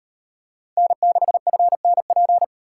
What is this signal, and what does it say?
Morse code, heard as a keyed tone, about 37 wpm, 700 Hz. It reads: N6FNP